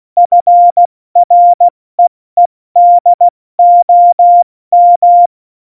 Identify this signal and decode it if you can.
Morse code, heard as a keyed tone, about 16 wpm, 700 Hz. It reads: FREEDOM
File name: Morsekurs / Witze / Witz 25 Juli 17, Tempo 70 F80